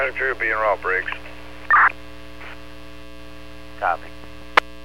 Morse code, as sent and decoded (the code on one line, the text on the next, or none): none